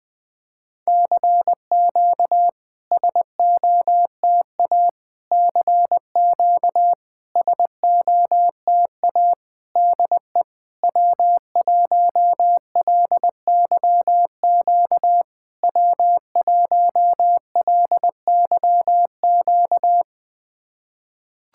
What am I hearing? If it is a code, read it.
Morse code, heard as a keyed tone, about 21 wpm, 700 Hz. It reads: CQ SOTA CQ SOTA DE W1LYQ W1LYQ
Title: SOTA QSO Practice 20wpm